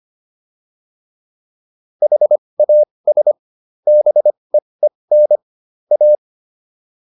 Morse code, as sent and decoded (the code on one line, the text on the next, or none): .... .- ... / -... . -. / .-
HAS BEEN A